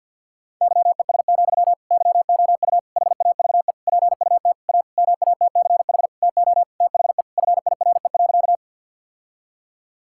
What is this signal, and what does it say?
Morse code, heard as a keyed tone, about 52 wpm, 700 Hz. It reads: YES, YOU HAVE PUT A MATCH TO THE FIRE.
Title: Sentences from Top 1600 Words 50wpm